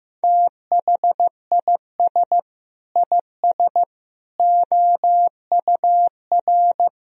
Morse code, heard as a keyed tone, about 15 wpm, 700 Hz. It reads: THIS IS OUR
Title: Sentences from Top 1000 Words - Review Part 4 - 15wpm